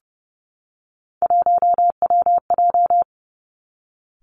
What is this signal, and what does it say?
Morse code, heard as a keyed tone, about 31 wpm, 700 Hz. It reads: 1WJ
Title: Three Challenging Letter-Number Combinations 30wpm